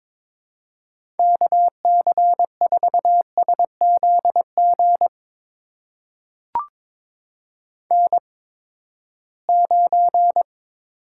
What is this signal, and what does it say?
Morse code, heard as a keyed tone, about 22 wpm, spaced as for 15 wpm, 700 Hz. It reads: KC4SZG N9
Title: USA Call Signs - 8x Character Spacing - 22wpm